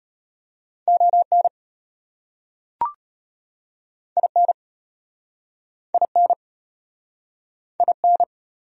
Morse code, heard as a keyed tone, about 34 wpm, 700 Hz. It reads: ON IN IN IN